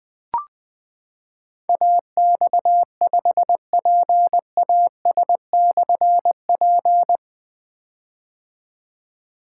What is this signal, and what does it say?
Morse code, heard as a keyed tone, about 20 wpm, 700 Hz. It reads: AX5PAS/P
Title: Difficult Call Signs 20wpm